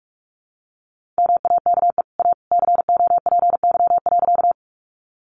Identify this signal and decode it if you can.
Morse code, heard as a keyed tone, about 44 wpm, 700 Hz. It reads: MAKE A COPY.
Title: Sentences from Top 1400 Words 45wpm